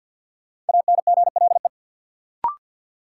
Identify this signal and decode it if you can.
Morse code, heard as a keyed tone, about 53 wpm, 700 Hz. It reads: ANGLE
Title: Sentences from Top 1500 Words - New Word Drill - 50wpm